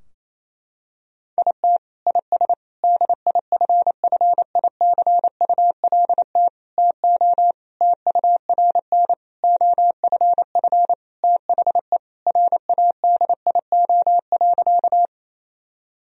Morse code, heard as a keyed tone, about 28 wpm, 700 Hz. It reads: IT IS DIFFICULT TO TURN OFF THE RADIO.
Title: Sentences from Top 1000 Words - Review Part 3 - 28wpm